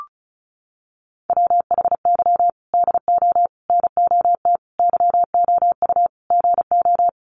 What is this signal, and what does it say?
Morse code, heard as a keyed tone, about 34 wpm, 700 Hz. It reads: WHY DO NOT YOU GO
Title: Sets of 5 Words Encore 35wpm